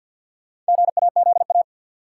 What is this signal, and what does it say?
Morse code, heard as a keyed tone, about 53 wpm, 700 Hz. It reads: GAZA
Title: Sets of 5 Words - New Word Drill - 50wpm